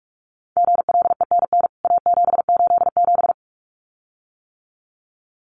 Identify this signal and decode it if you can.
Morse code, heard as a keyed tone, about 44 wpm, 700 Hz. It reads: GLENN A787
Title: CWT Exchange 45wpm